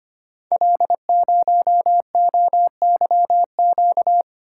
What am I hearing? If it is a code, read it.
Morse code, heard as a keyed tone, about 25 wpm, 700 Hz. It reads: L0OYQ
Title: Code Groups - 5 Letter:Numbers - 25wpm